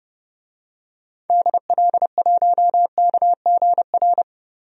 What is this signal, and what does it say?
Morse code, heard as a keyed tone, about 30 wpm, 700 Hz. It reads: DL1KGR
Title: DXCC-100 Call Signs - 6 Characters - Repeated 3x - 30wpm